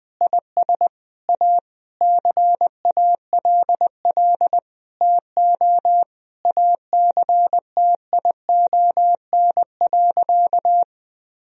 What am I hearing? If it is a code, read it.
Morse code, heard as a keyed tone, about 20 wpm, 700 Hz. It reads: IS A CALL TO ACTION.
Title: Sentences from Top 1000 Words - Review Part 4 - 20wpm